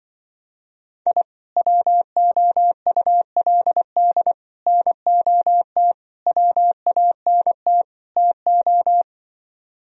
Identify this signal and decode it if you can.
Morse code, heard as a keyed tone, about 24 wpm, 700 Hz. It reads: I WOULD NOT WANT TO
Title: Speed-Racing - Sets of 5 Words 20wpm